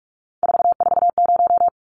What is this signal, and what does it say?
Morse code, heard as a keyed tone, about 44 wpm, 700 Hz. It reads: 440
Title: Numbers - 3-Digit 45wpm